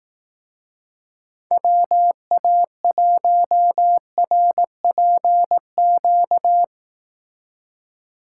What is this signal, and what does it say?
Morse code, heard as a keyed tone, about 18 wpm, 700 Hz. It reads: WA1RPQ
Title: Speed Racing - USA 2x3 Call Signs 15wpm